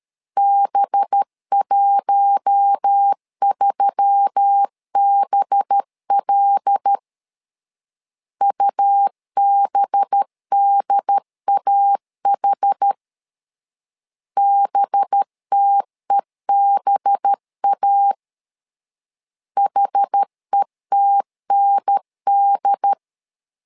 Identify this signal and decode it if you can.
Morse code, heard as a keyed tone, about 13 wpm, 800 Hz. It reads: B13BL UBDAH BTEBA HETND
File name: LEZIONI / 5 LBVC / 5 LB 12wpm 1906